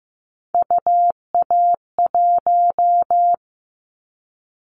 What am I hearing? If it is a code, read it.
Morse code, heard as a keyed tone, about 15 wpm, 700 Hz. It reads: UA1
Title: Callsign Prefixes for Top 25 DXCC - 15wpm